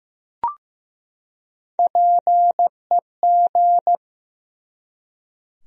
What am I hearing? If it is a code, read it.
Morse code, heard as a keyed tone, about 15 wpm, 700 Hz. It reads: PEG
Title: USA Common Names - Max 5 Letters - 15wpm